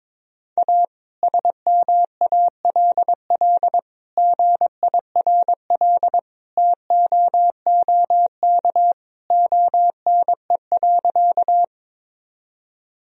Morse code, heard as a keyed tone, about 22 wpm, 700 Hz. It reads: A SMALL GIRL TOOK ONE.
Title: Sentences from Top 1000 Words - Review Part 1 - 22wpm